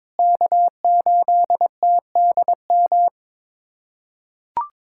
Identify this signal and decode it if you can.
Morse code, heard as a keyed tone, about 22 wpm, 700 Hz. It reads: K8TDM
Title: USA Callsign DE Callsign - 5 Characters - 22wpm